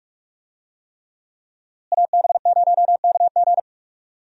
Morse code, heard as a keyed tone, about 45 wpm, 700 Hz. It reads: AB0KG
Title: USA Call Signs - 5 Characters - Repeated 3x - 45wpm